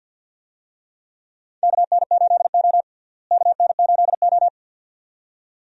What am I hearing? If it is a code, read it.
Morse code, heard as a keyed tone, about 52 wpm, 700 Hz. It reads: KN8O KN8O
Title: USA Call Signs - 4 Characters - Repeated 2x - 50wpm